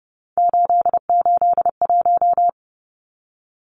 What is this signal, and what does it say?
Morse code, heard as a keyed tone, about 30 wpm, 700 Hz. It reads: 881
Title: Numbers - 3-Digit 30wpm